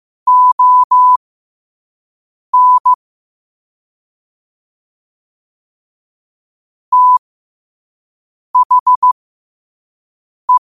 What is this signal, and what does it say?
Morse code, heard as a keyed tone, about 15 wpm, spaced as for 5 wpm, 1000 Hz. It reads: ON THE
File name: qotdfiles / 5wpm